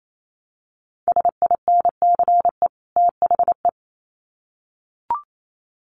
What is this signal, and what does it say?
Morse code, heard as a keyed tone, about 28 wpm, 700 Hz. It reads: SINCE THE
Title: Sets of 2 Words 28wpm